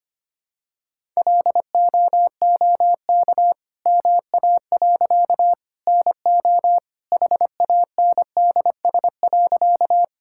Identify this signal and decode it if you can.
Morse code, heard as a keyed tone, about 25 wpm, 700 Hz. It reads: LOOK MA. NO HANDS.
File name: practice / 25wpm / All-Age Appropriate Jokes - 25wpm